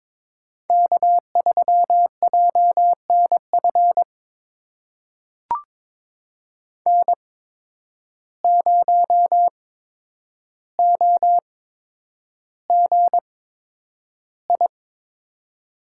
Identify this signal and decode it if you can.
Morse code, heard as a keyed tone, about 23 wpm, spaced as for 10 wpm, 700 Hz. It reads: K3JNF N0OGI
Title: USA Call Signs - 5 Characters - 8x Character Spacing - 22wpm